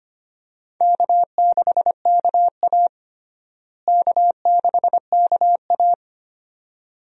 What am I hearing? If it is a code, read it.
Morse code, heard as a keyed tone, about 25 wpm, 700 Hz. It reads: K6KA K6KA
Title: USA Call Signs - 4 Characters - Repeated 2x - 25wpm